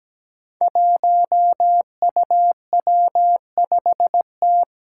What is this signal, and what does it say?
Morse code, heard as a keyed tone, about 17 wpm, 700 Hz. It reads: 1UW5T
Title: Five Challenging Letter-Number Combinations 17wpm